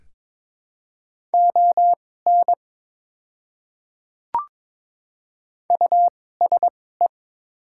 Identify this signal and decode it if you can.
Morse code, heard as a keyed tone, about 22 wpm, spaced as for 16 wpm, 700 Hz. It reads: ON USE
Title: 2x Character Spacing - Top 300 Words - 22wpm